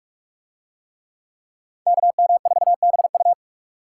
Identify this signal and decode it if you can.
Morse code, heard as a keyed tone, about 45 wpm, 700 Hz. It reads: KM4BU